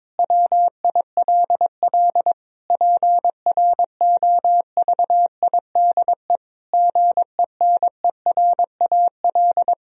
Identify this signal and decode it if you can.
Morse code, heard as a keyed tone, about 22 wpm, 700 Hz. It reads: WILL PROVIDE GENERAL
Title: Sentences from Top 1000 Words - Review Part 4 - 22wpm